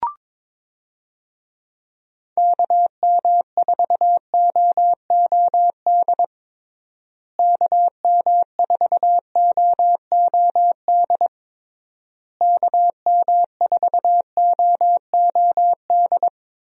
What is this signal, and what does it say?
Morse code, heard as a keyed tone, about 21 wpm, 700 Hz. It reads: KM4OOD KM4OOD KM4OOD
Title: USA Call Signs - 6 Characters - Repeated 3x - 22wpm